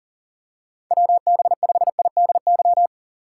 Morse code, heard as a keyed tone, about 40 wpm, 700 Hz. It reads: WB5IDY
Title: K1USN SST QSO Practice - 40wpm